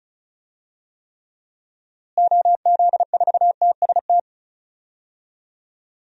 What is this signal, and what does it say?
Morse code, heard as a keyed tone, about 34 wpm, 700 Hz. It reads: OZ4TST
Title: DXCC-100 Call Signs - 6 Characters - Repeated 3x - 35wpm